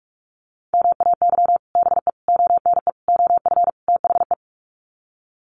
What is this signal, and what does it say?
Morse code, heard as a keyed tone, about 44 wpm, 690 Hz. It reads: MAY BE ONE OF THE
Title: Sets of 5 Words Encore2 45wpm